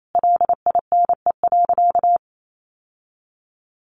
Morse code, heard as a keyed tone, about 28 wpm, 700 Hz. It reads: LINE.
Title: Sentences from Top 1200 Words 28wpm